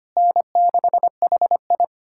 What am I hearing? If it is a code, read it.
Morse code, heard as a keyed tone, about 25 wpm, 700 Hz. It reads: N6HI
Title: USA Callsign DE Callsign - 4 Characters - 25wpm